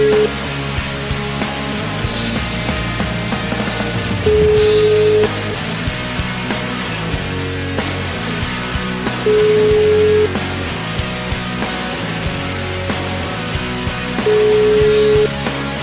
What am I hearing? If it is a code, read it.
Ringback tone (Europe, 425 Hz), with noise.